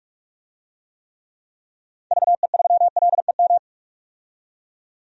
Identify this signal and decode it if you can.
Morse code, heard as a keyed tone, about 42 wpm, 700 Hz. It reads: VE3LEM
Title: DXCC-100 Call Signs - 6 Characters - 45wpm